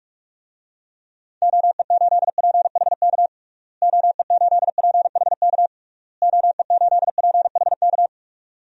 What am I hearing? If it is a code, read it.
Morse code, heard as a keyed tone, about 41 wpm, 700 Hz. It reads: OE8PHK OE8PHK OE8PHK
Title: DXCC-100 Call Signs - 6 Characters - Repeated 3x - 45wpm